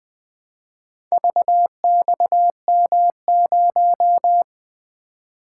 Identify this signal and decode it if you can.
Morse code, heard as a keyed tone, about 20 wpm, 700 Hz. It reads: VXM0
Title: Four Letter-Number Combinations - 20wpm